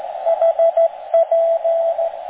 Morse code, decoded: HP